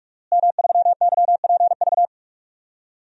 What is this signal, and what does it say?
Morse code, heard as a keyed tone, about 46 wpm, 700 Hz. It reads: M3YPV
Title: DXCC-100 Call Signs - 5 Characters - Repeated 2x - 45wpm